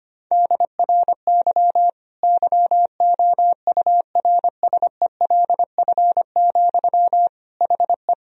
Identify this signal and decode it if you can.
Morse code, heard as a keyed tone, about 25 wpm, 700 Hz. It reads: DRY YOURSELF, HE
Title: Sentences from Top 1000 Words - Review Part 3 - 25wpm